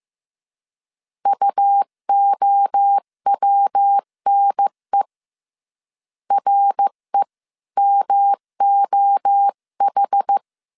Morse code, heard as a keyed tone, about 15 wpm, 800 Hz. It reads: UOWNE REMOH
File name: CW / DET / 1 3 1500C S A 14wpm 2447